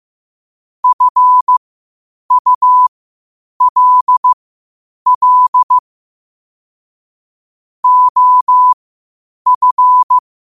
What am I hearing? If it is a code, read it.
Morse code, heard as a keyed tone, about 15 wpm, spaced as for 8 wpm, 1000 Hz. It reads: FULL OF
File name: qotdfiles / 8wpm